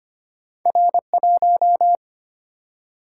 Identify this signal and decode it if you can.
Morse code, heard as a keyed tone, about 25 wpm, 700 Hz. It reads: R1
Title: Single-Letter Single-Number 25wpm